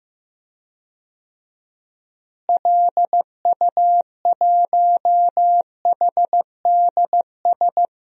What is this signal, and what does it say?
Morse code, heard as a keyed tone, about 15 wpm, 700 Hz. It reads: LU1HDS
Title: DXCC-100 Call Signs - 6 Characters - Repeated 3x - 15wpm